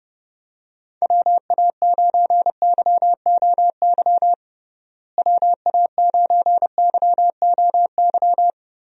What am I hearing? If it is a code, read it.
Morse code, heard as a keyed tone, about 30 wpm, 700 Hz. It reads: WA9YOY WA9YOY
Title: USA Call Signs - 6 Characters - Repeated 2x - 30wpm